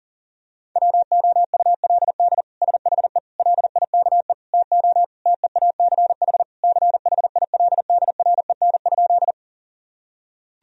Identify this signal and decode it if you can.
Morse code, heard as a keyed tone, about 37 wpm, 700 Hz. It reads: WOULD SHE LIKE TO TEACH CHILDREN?
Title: Sentences from Top 1000 Words - Review Part 2 - 40wpm